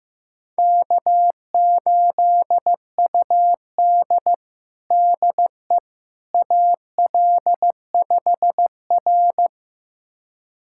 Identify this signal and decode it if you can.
Morse code, heard as a keyed tone, about 15 wpm, 700 Hz. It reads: K8UD DE AL5R
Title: USA Callsign DE Callsign - 4 Characters - 15wpm